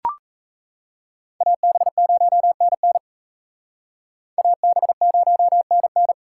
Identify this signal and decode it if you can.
Morse code, heard as a keyed tone, about 35 wpm, 700 Hz. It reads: AB0NN AB0NN
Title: Speed Racing - USA 2x2 Call Signs 28wpm